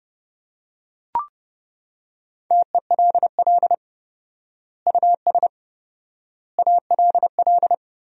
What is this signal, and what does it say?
Morse code, heard as a keyed tone, about 30 wpm, 700 Hz. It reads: TELL US ALL
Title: 4x Word-Spacing - Sentences from Top 200 Words - 30wpm